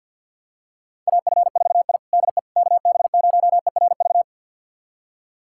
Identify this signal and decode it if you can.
Morse code, heard as a keyed tone, about 50 wpm, 700 Hz. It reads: AF4I DE KB0ERV
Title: USA Callsign DE Callsign 50wpm